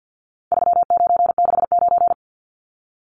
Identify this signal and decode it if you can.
Morse code, heard as a keyed tone, about 54 wpm, 700 Hz. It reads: 3969
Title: Numbers - 4-Digit 50wpm